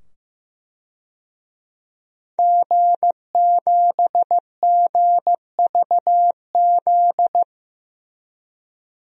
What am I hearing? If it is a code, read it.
Morse code, heard as a keyed tone, about 15 wpm, 700 Hz. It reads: G7GVZ
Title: DXCC-100 Call Signs - 5 Characters - 15wpm